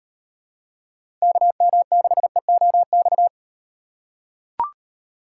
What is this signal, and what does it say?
Morse code, heard as a keyed tone, about 35 wpm, 700 Hz. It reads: KM6EOX